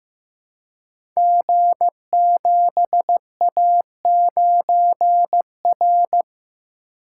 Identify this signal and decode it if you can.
Morse code, heard as a keyed tone, about 15 wpm, 700 Hz. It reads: G7A9R
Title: Code Groups - 5 Letter:Numbers - 15wpm